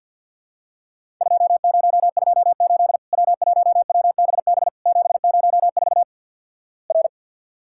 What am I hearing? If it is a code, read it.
Morse code, heard as a keyed tone, about 50 wpm, 700 Hz. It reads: 2028 W1WBB 704 R
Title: Worked All Europe DX Contest - QTC Exchange - 50wpm